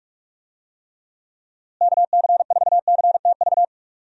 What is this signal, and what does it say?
Morse code, heard as a keyed tone, about 45 wpm, 700 Hz. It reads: KC4CTV
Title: USA Call Signs - 6 Characters - Repeated 2x - 45wpm